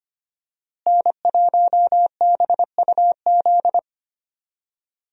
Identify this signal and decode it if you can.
Morse code, heard as a keyed tone, about 25 wpm, 690 Hz. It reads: N1BUZ
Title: USA Call Signs - 5 Characters - Repeated 2x - 25wpm